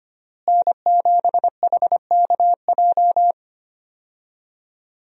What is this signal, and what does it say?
Morse code, heard as a keyed tone, about 25 wpm, 700 Hz. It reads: N7HKJ
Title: USA Call Signs - 5 Characters - Repeated 3x - 25wpm